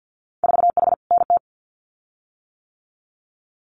Morse code, heard as a keyed tone, about 50 wpm, 700 Hz. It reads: 4H NT
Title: Winter Field Day Exchanges Encore 50wpm